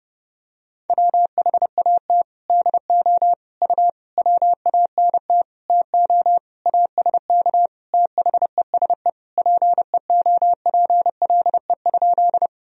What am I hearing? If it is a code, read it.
Morse code, heard as a keyed tone, about 30 wpm, 700 Hz. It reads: WHAT DO U WANT TO ASK THESE PEOPLE?
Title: Sentences from Top 200 Words and Abbreviations 30wpm